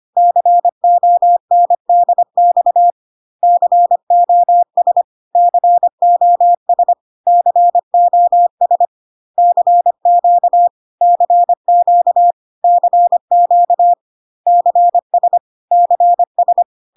Morse code, heard as a keyed tone, about 25 wpm, 700 Hz. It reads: CONDX COS COS COS CQ CQ CQ CS CS
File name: CW abbreviations alphabetical repeated 3 times 25 25wpm ik1hge